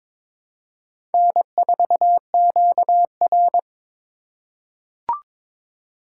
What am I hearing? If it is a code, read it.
Morse code, heard as a keyed tone, about 22 wpm, 700 Hz. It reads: N4QR
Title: USA Call Signs - 4 Characters - 2x Character Spacing - 22wpm